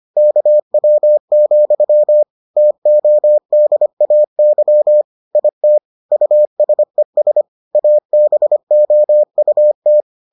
Morse code, heard as a keyed tone, about 25 wpm, 600 Hz. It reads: KW, TODAY IT USES ABOUT